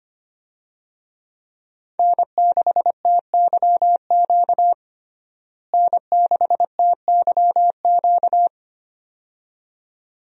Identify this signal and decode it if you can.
Morse code, heard as a keyed tone, about 25 wpm, 700 Hz. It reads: N6TYQ N6TYQ
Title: USA Call Signs - 5 Characters - Repeated 2x - 25wpm